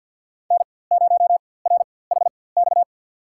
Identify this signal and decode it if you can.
Morse code, heard as a keyed tone, about 50 wpm, spaced as for 23 wpm, 700 Hz. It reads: N0RHX
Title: USA Call Signs - 5 Characters - 4x Character Spacing - 50wpm